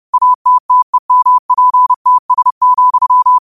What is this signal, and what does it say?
Morse code, heard as a keyed tone, about 30 wpm, 1000 Hz. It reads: ATTEMPTS,
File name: qotdfiles / 30wpm